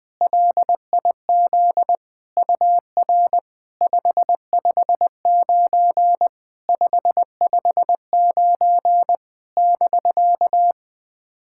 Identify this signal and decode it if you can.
Morse code, heard as a keyed tone, about 20 wpm, 700 Hz. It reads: LIZ UR 559 559 <BK>
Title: SOTA QSO Practice 20wpm